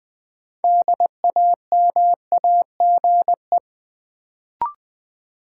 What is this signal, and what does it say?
Morse code, heard as a keyed tone, about 20 wpm, 700 Hz. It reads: DAMAGE
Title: Sentences from Top 1100 Words - New Word Drill - 20wpm